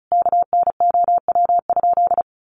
Morse code, heard as a keyed tone, about 34 wpm, 700 Hz. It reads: KNOW?
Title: Sentences from Top 100 Words 35wpm